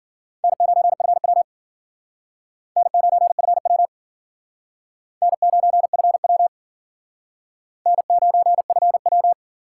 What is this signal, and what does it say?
Morse code, heard as a keyed tone, about 53 wpm, 700 Hz. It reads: N9FW N9FW N9FW N9FW